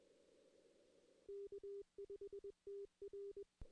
Morse code, heard as a keyed tone, about 21 wpm, 400 Hz. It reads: K5TR
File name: 2016 arrl dx cw / 2016-ARRL-DX-CW-K5TR- - 0-3